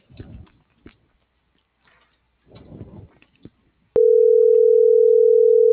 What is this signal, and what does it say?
Ringback tone (North America), 440 + 480 Hz.